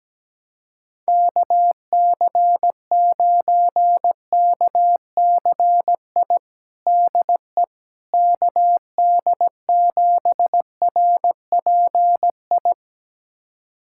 Morse code, heard as a keyed tone, about 17 wpm, 700 Hz. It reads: KC9KCI DE KD7RPI
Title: USA Callsign DE Callsign 17wpm